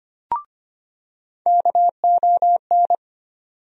Morse code, heard as a keyed tone, about 25 wpm, 700 Hz. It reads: KON
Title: USA Call Sign Postfixes - 3 Letters 25wpm